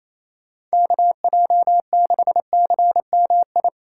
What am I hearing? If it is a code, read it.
Morse code, heard as a keyed tone, about 34 wpm, 700 Hz. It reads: KJ6CMI
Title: Speed Racing - USA Call Signs 28wpm